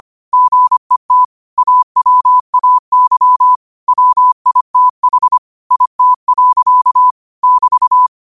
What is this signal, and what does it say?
Morse code, heard as a keyed tone, about 25 wpm, 1000 Hz. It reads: GET AWAY WITH IT. <BT>